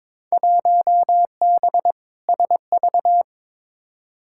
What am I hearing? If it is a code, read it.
Morse code, heard as a keyed tone, about 22 wpm, 700 Hz. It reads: 1B SV
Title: Field Day Exchanges - 22wpm